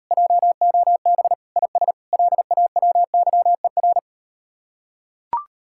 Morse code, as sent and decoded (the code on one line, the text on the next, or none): .--- --- -... / .. ... / .-.. .- .-- -.-- . .-.
JOB IS LAWYER